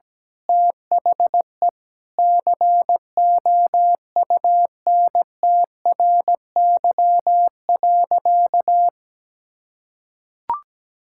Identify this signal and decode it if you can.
Morse code, heard as a keyed tone, about 17 wpm, 700 Hz. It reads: THE COUNTRY.
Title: Sentences from Top 1000 Words - Review Part 5 - 17wpm